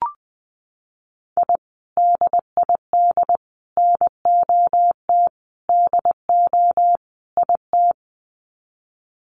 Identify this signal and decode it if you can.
Morse code, heard as a keyed tone, about 20 wpm, 700 Hz. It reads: I DID NOT DO IT
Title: Sets of 5 Words Encore2 20wpm